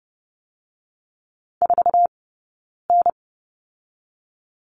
Morse code, heard as a keyed tone, about 30 wpm, 700 Hz. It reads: 4 N
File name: 092 Lesson 16d - Letter c - Two Characters 30wpm